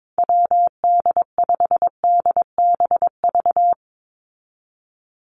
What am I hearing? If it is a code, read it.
Morse code, heard as a keyed tone, about 22 wpm, 700 Hz. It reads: WD5DBV
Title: Call Signs Repeated 2x - 22wpm